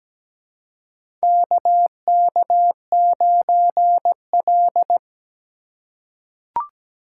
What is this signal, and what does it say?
Morse code, heard as a keyed tone, about 17 wpm, 700 Hz. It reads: KK9L
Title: USA Call Signs - 4 Characters - 3x Character Spacing - 17wpm